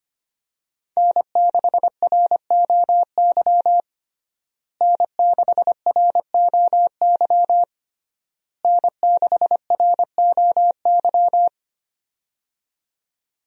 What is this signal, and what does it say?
Morse code, heard as a keyed tone, about 25 wpm, 700 Hz. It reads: N6ROY N6ROY N6ROY
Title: USA Call Signs - 5 Characters - Repeated 3x - 25wpm